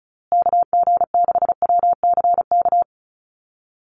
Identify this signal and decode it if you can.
Morse code, heard as a keyed tone, about 34 wpm, 700 Hz. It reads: KG6WCK
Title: USA Call Signs Repeated 3x 35wpm